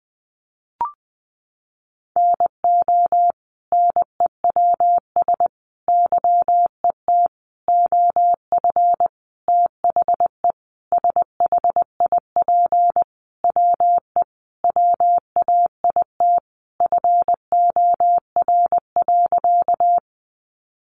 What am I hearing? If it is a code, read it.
Morse code, heard as a keyed tone, about 20 wpm, 700 Hz. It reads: NO NEWS YET OF THE SHIP WE WAIT FOR.
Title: Sentences from Top 800 Words 20wpm